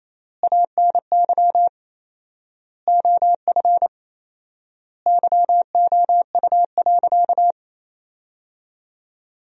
Morse code, heard as a keyed tone, about 28 wpm, 700 Hz. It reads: ANY OF YOU.
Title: 4x Word-Spacing - Sentences from Top 400 Words - 28wpm